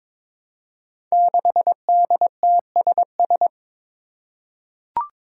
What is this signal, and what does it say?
Morse code, heard as a keyed tone, about 22 wpm, 700 Hz. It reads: 6DTSS
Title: Five Challenging Letter-Number Combinations 22wpm